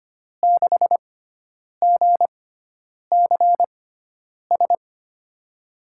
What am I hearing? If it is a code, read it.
Morse code, heard as a keyed tone, about 25 wpm, spaced as for 9 wpm, 700 Hz. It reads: 6GCS
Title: USA Call Signs - 6 Characters - 6x Character Spacing - 25wpm